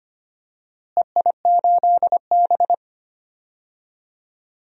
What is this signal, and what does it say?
Morse code, heard as a keyed tone, about 25 wpm, 700 Hz. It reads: EI8B
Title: DXCC-100 Call Signs - 4 Characters - Repeated 3x - 25wpm